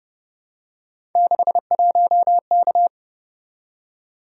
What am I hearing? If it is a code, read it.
Morse code, heard as a keyed tone, about 29 wpm, 700 Hz. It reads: 61K